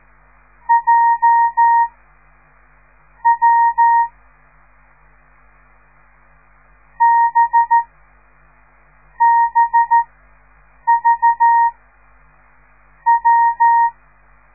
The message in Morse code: J W B BV W